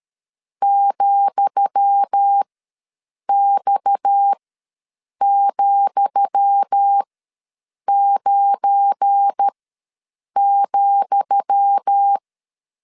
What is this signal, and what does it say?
Morse code, heard as a keyed tone, about 13 wpm, spaced as for 7 wpm, 800 Hz. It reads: ,X,9,
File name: LEZIONI / 10 SIGNS / 10D COMA ws 12wpm 3618